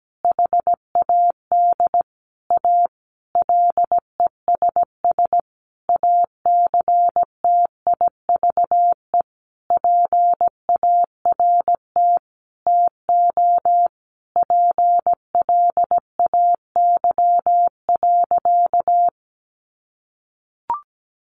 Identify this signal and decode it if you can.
Morse code, heard as a keyed tone, about 17 wpm, 700 Hz. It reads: HAD A LESS ACTIVE PART TO PLAY.
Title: Sentences from Top 900 Words 17wpm